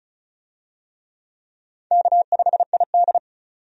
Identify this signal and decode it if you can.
Morse code, heard as a keyed tone, about 36 wpm, 700 Hz. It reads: K5ID
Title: USA Call Signs - 4 Characters - 35wpm